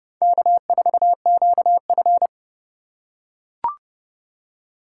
Morse code, heard as a keyed tone, about 30 wpm, 700 Hz. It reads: K4QF
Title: USA Call Signs - 4 Characters - 3x Character Spacing - 30wpm